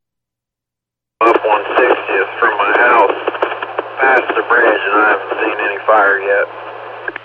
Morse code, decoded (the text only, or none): none